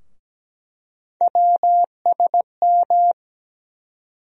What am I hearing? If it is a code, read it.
Morse code, heard as a keyed tone, about 17 wpm, 700 Hz. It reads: WSM
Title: USA Call Sign Postfixes - 3 Letters 17wpm